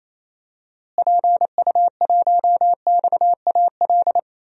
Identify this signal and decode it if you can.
Morse code, heard as a keyed tone, about 28 wpm, 700 Hz. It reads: PU1XAL
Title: DXCC-100 Call Signs - 6 Characters - Repeated 3x - 28wpm